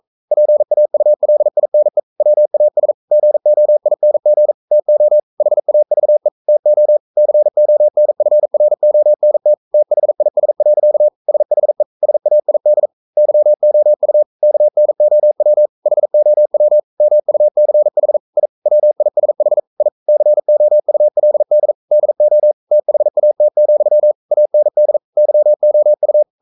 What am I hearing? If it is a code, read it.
Morse code, heard as a keyed tone, about 42 wpm, 600 Hz. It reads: PAULINE WAS GOING TO HAVE TO CONFRONT THIS. SHE SAID YOU KNOW HOW MUCH I WISH I COULD DO THAT, AND YOU
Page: Romanen Never af Ken Follet på engelsk.